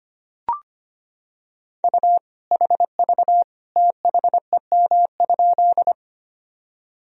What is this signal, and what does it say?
Morse code, heard as a keyed tone, about 25 wpm, 700 Hz. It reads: U HV THEM?
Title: Sentences from Top 100 Words and Abbreviations 25wpm